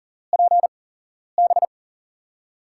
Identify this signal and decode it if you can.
Morse code, heard as a keyed tone, about 40 wpm, spaced as for 11 wpm, 700 Hz. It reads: PB
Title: USA Call Signs - 6 Characters - 8x Character Spacing - 40wpm